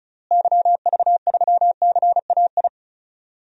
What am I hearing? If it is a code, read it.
Morse code, heard as a keyed tone, about 34 wpm, 700 Hz. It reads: YV3CAI
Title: DXCC-100 Call Signs - 6 Characters - Repeated 3x - 35wpm